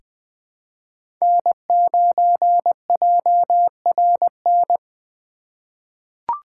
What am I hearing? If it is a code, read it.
Morse code, heard as a keyed tone, about 20 wpm, 700 Hz. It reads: N9JRN
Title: USA Call Signs - 5 Characters - 3x Character Spacing - 20wpm